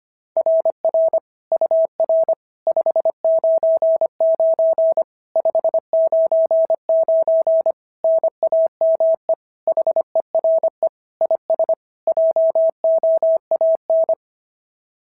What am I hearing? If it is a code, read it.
Morse code, heard as a keyed tone, about 25 wpm, 690 Hz. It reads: RR UR 599 599 NAME HERE IS JOAN
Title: SOTA QSO Practice 25wpm